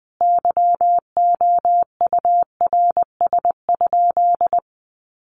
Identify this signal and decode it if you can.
Morse code, heard as a keyed tone, about 20 wpm, 700 Hz. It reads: YOURS?
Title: Sentences from Top 1500 Words 20wpm